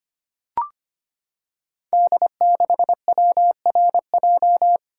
Morse code, heard as a keyed tone, about 25 wpm, 700 Hz. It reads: D6WRJ
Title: Code Groups - 5 Letter:Numbers - 25wpm